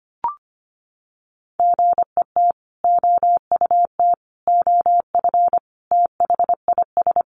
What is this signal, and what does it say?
Morse code, heard as a keyed tone, about 25 wpm, 700 Hz. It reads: GET OUT OF THIS
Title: Sentences from Top 1000 Words 25wpm